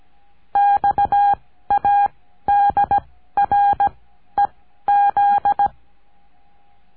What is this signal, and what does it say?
Morse code, heard as a keyed tone, about 17 wpm, spaced as for 12 wpm, 810 Hz. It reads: XADREZ
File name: cursos / CURSO DE TELEGRAFIA / Telegrafia - Trilha 14